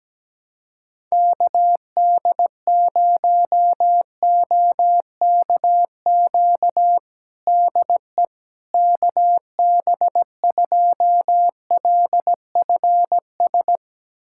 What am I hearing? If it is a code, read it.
Morse code, heard as a keyed tone, about 17 wpm, 700 Hz. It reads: KD0OKQ DE KB2LFS
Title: USA Callsign DE Callsign - 6 Characters - 17wpm